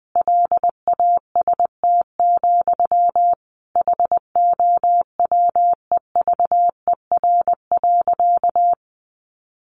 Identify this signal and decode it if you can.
Morse code, heard as a keyed tone, about 20 wpm, 700 Hz. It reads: LAST, HOWEVER.
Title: Sentences from Top 600 Words 20wpm